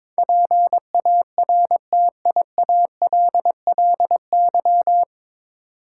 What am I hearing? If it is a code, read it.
Morse code, heard as a keyed tone, about 22 wpm, 700 Hz. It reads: PARTIALLY
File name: 9 Letter Words - First 3 Letters Given - 22wpm